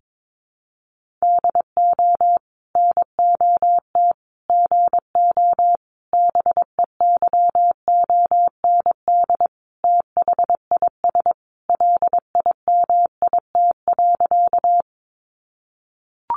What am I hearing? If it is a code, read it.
Morse code, heard as a keyed tone, about 22 wpm, 700 Hz. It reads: DO NOT GO BEYOND THIS LIMIT.
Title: Sentences from Top 1500 Words 22wpm